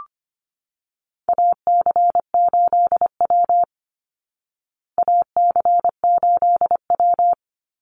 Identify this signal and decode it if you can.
Morse code, heard as a keyed tone, about 25 wpm, 700 Hz. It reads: AC8W AC8W
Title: Call Signs Repeated 3x - 25wpm